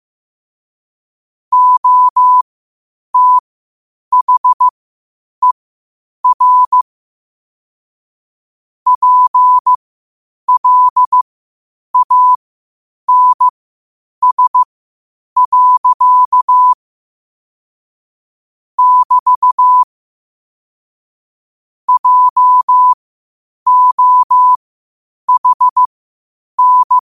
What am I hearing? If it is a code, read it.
Morse code, heard as a keyed tone, about 15 wpm, spaced as for 8 wpm, 1000 Hz. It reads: OTHER PLANS. <BT> JOHN
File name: qotdfiles / 8wpm